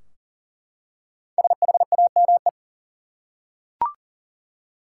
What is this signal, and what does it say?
Morse code, heard as a keyed tone, about 40 wpm, 700 Hz. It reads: SHAME
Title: Five Letter Words - 40wpm